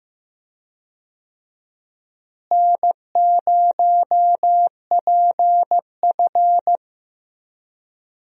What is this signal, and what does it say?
Morse code, heard as a keyed tone, about 15 wpm, 700 Hz. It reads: N0PF
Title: USA Call Signs - 4 Characters - Repeated 2x - 15wpm